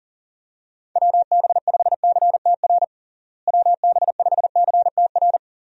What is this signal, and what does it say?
Morse code, heard as a keyed tone, about 37 wpm, 700 Hz. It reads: WB5CTR WB5CTR
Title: USA Call Signs - 6 Characters - Repeated 2x - 40wpm